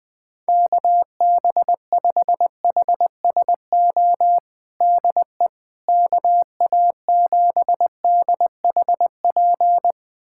Morse code, heard as a keyed tone, about 20 wpm, 700 Hz. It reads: KB5HSO DE KA7DHP
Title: USA Callsign DE Callsign 20wpm